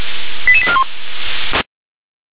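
Five-tone selective call (ZVEI), heard as quick runs of five tones, tones of about 70 ms. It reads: 88841
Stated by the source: ZVEI five-tone selective call received by SDR.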